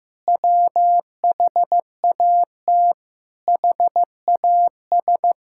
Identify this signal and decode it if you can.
Morse code, heard as a keyed tone, about 14 wpm, 700 Hz. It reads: WHAT HAS
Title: Sentences from Top 1000 Words - Review Part 5 - 15wpm